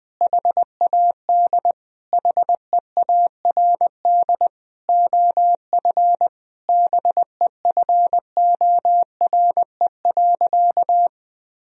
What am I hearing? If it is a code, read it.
Morse code, heard as a keyed tone, about 21 wpm, 700 Hz. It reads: HAD HEARD OF BEFORE.
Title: Sentences from Top 1000 Words - Review Part 3 - 20wpm